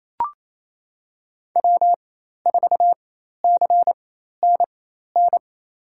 Morse code, heard as a keyed tone, about 28 wpm, spaced as for 13 wpm, 700 Hz. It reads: W4CNN
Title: USA Call Signs - 5 Characters - 4x Character Spacing - 28wpm